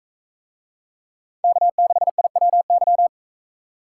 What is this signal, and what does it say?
Morse code, heard as a keyed tone, about 39 wpm, 700 Hz. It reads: K6IWY